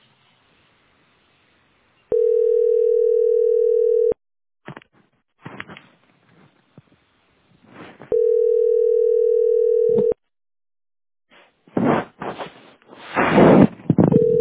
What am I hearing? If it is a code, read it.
Ringback tone (North America), 440 + 480 Hz.